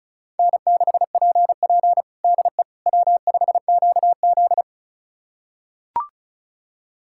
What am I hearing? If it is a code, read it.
Morse code, heard as a keyed tone, about 34 wpm, 700 Hz. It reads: N6PP DE W5QZ
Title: USA Callsign DE Callsign - 4 Characters - 35wpm